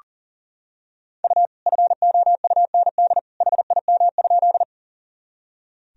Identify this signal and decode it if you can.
Morse code, heard as a keyed tone, about 40 wpm, 700 Hz. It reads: U FOUND HIM?
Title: Sentences from Top 300 Words and Abbreviations 40wpm